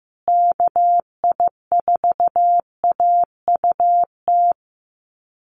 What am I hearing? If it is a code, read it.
Morse code, heard as a keyed tone, about 15 wpm, 700 Hz. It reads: KI4AUT
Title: USA Call Signs Repeated 2x 15wpm